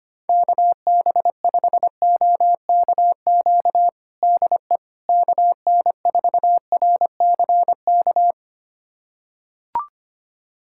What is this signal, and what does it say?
Morse code, heard as a keyed tone, about 25 wpm, 700 Hz. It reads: KB5OKQ DE KN4RCK
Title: USA Callsign DE Callsign 25wpm